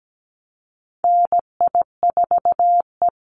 Morse code, heard as a keyed tone, about 17 wpm, 700 Hz. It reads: NI4E
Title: Call Signs - 17wpm